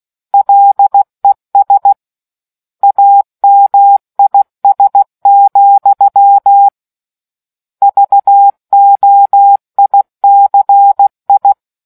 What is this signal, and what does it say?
Morse code, heard as a keyed tone, about 16 wpm, 800 Hz. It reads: LES AMIS, VOICI